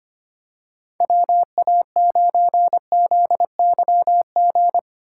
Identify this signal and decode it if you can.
Morse code, heard as a keyed tone, about 25 wpm, 700 Hz. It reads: WA9ZYG
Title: USA Call Signs - 6 Characters - 4x Character Spacing - 25wpm